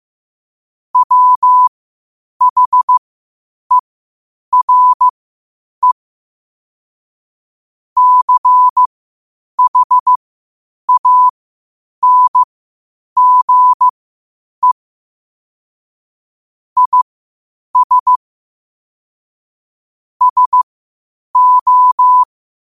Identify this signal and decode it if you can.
Morse code, heard as a keyed tone, about 15 wpm, spaced as for 8 wpm, 1000 Hz. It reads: WHERE CHANGE IS SO
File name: qotdfiles / 8wpm